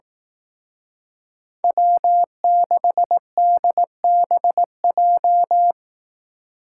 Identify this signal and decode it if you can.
Morse code, heard as a keyed tone, about 18 wpm, 700 Hz. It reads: W6DBJ
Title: Speed Racing - USA 1x3 Call Signs 15wpm